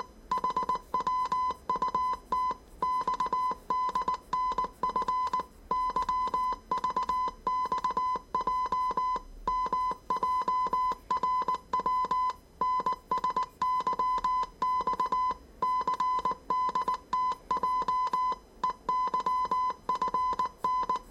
Morse code, heard as a keyed tone, about 19 wpm, 1010 Hz. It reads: HWUT XDNF YVXJ MJRW NSYX CDTJ EYFN
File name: 业余无线电 / ●未分类 / 字码80